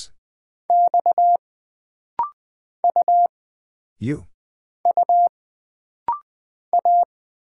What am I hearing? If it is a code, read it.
Morse code, heard as a keyed tone, about 20 wpm, 700 Hz. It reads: X U U A
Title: Single Character (Letter:Number:Punctuation:Prosign) - Rapid-Fire - 20wpm